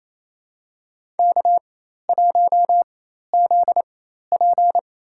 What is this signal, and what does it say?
Morse code, heard as a keyed tone, about 28 wpm, spaced as for 13 wpm, 700 Hz. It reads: K1ZP
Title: USA Call Signs - 4 Characters - 4x Character Spacing - 28wpm